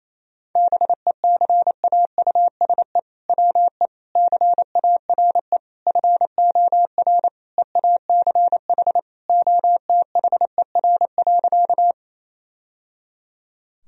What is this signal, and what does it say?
Morse code, heard as a keyed tone, about 28 wpm, 700 Hz. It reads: BECAUSE WE CARE FOR EACH OTHER.
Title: Sentences from Top 1000 Words - Review Part 3 - 28wpm